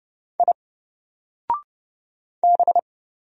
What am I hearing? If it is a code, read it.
Morse code, heard as a keyed tone, about 30 wpm, 700 Hz. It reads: I B
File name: Single-Letter-Number-30wpm